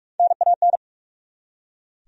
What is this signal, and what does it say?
Morse code, heard as a keyed tone, about 46 wpm, 700 Hz. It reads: NAN
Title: USA Common Names - Max 4 Letters - 45wpm